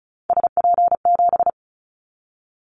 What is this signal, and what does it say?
Morse code, heard as a keyed tone, about 34 wpm, 700 Hz. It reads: SP7
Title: Callsign Prefixes for Top 25 DXCC - 35wpm